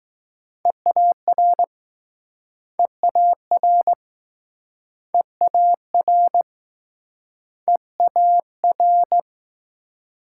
Morse code, heard as a keyed tone, about 21 wpm, 700 Hz. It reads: EAR EAR EAR EAR